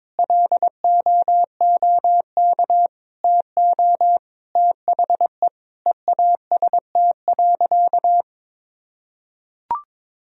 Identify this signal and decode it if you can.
Morse code, heard as a keyed tone, about 22 wpm, 700 Hz. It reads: LOOK TO THE EAST.
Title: Sentences from Top 1000 Words - Review Part 3 - 22wpm